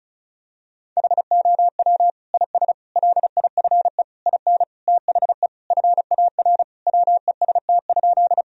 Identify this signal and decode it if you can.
Morse code, heard as a keyed tone, about 34 wpm, 700 Hz. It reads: HOW IS LIFE IN THE FAR WEST?
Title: Sentences from Top 1000 Words - Review Part 2 - 35wpm